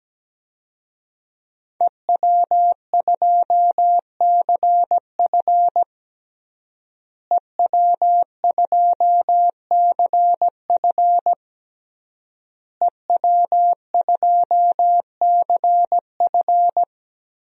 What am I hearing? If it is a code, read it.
Morse code, heard as a keyed tone, about 17 wpm, 700 Hz. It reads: EW2CF EW2CF EW2CF
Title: DXCC-100 Call Signs - 5 Characters - Repeated 3x - 17wpm